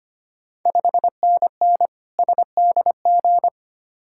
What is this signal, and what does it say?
Morse code, heard as a keyed tone, about 25 wpm, 700 Hz. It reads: 5NN SDG
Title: Ontario QSO Party Exchange 25wpm